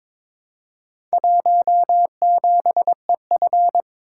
Morse code, heard as a keyed tone, about 22 wpm, 700 Hz. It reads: 17EF
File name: Four Letter-Number Combinations - 22wpm